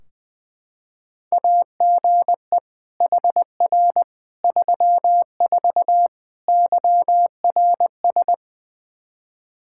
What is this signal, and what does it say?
Morse code, heard as a keyed tone, about 20 wpm, 700 Hz. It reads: AGE HR 34 YRS
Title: Practice QSO - Exchange 3 - More Info - 20wpm